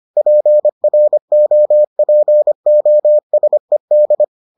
Morse code, heard as a keyed tone, about 25 wpm, 600 Hz. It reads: PROPOSED